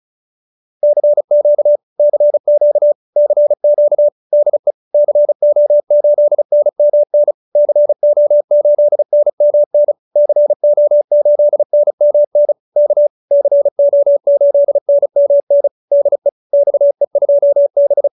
Morse code, heard as a keyed tone, about 34 wpm, 590 Hz. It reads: CQ CQ CQ DE CO8NMN CO8NMN CO8NMN K CO8NMN DE XE2B
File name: QSO304 35